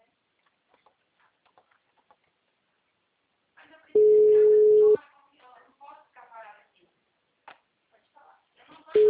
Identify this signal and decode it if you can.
Ringback tone (Europe, 425 Hz), with speech.